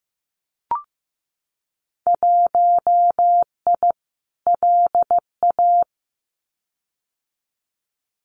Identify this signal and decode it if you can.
Morse code, heard as a keyed tone, about 15 wpm, 700 Hz. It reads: 1I LA
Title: Winter Field Day Exchanges Encore 15wpm